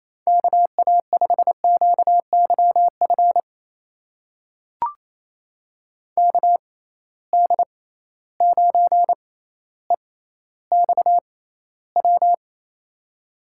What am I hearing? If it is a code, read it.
Morse code, heard as a keyed tone, about 28 wpm, spaced as for 14 wpm, 700 Hz. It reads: KA5QYF KD9EXW
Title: USA Call Signs - 6 Characters - 6x Character Spacing - 28wpm